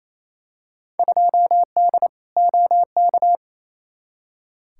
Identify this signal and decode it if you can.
Morse code, heard as a keyed tone, about 28 wpm, 700 Hz. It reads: 2D OK
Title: Field Day Exchanges Encore - 28wpm